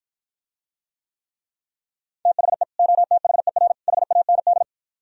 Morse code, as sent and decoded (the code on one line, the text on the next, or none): - .... / --- - .... .-. / .... .- -. -..
THE OTHER HAND